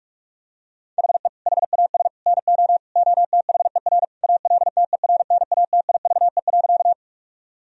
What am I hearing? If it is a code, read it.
Morse code, heard as a keyed tone, about 41 wpm, 700 Hz. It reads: HE HAS NO OTHER ALTERNATIVE.
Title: Sentences from Top 1000 Words - Review Part 4 - 45wpm